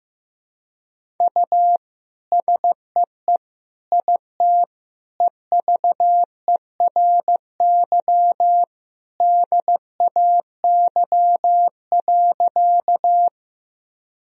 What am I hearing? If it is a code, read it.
Morse code, heard as a keyed tone, about 16 wpm, 700 Hz. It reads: U SEE IT EVERY DAY.
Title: Sentences from Top 200 Words and Abbreviations 15wpm